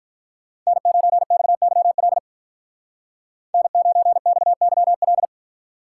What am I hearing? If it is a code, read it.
Morse code, heard as a keyed tone, about 43 wpm, 700 Hz. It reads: N9XYL N9XYL
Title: Speed Racing - USA 1x3 Call Signs 35wpm